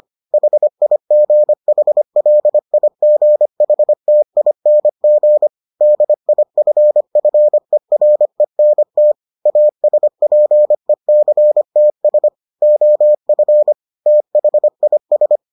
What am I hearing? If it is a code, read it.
Morse code, heard as a keyed tone, about 25 wpm, 600 Hz. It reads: HIGHLIGHTING DIFFERENT ASPECTS OF THIS